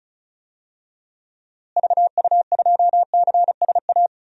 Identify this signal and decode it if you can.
Morse code, heard as a keyed tone, about 36 wpm, 700 Hz. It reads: VU2CSA
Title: DXCC-100 Call Signs - 6 Characters - Repeated 2x - 35wpm